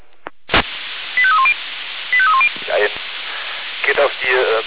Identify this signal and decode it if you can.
Five-tone selective call (ZVEI), heard as quick runs of five tones, tones of about 70 ms, with speech. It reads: 85310 85310